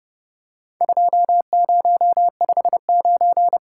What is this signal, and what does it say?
Morse code, heard as a keyed tone, about 30 wpm, 700 Hz. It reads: 2059
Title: Worked All Europe DX Contest - QTC Exchange - 30wpm